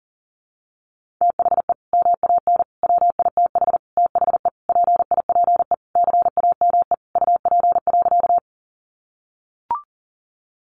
Morse code, heard as a keyed tone, about 40 wpm, 700 Hz. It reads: THE MAN WITH THE PIPE CAME UP.
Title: Sentences from Top 1500 Words 40wpm